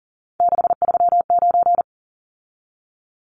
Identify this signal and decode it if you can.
Morse code, heard as a keyed tone, about 40 wpm, 700 Hz. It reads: 639
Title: Numbers - 3-Digit 40wpm